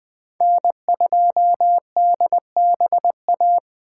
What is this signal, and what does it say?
Morse code, heard as a keyed tone, about 19 wpm, 700 Hz. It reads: N2DBA